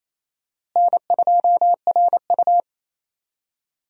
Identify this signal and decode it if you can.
Morse code, heard as a keyed tone, about 27 wpm, 700 Hz. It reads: N2RU
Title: USA Call Signs - 4 Characters - Repeated 3x - 28wpm